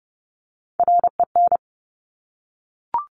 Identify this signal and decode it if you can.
Morse code, heard as a keyed tone, about 30 wpm, 700 Hz. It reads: REN
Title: Top Three Letter Combinations in Words 30wpm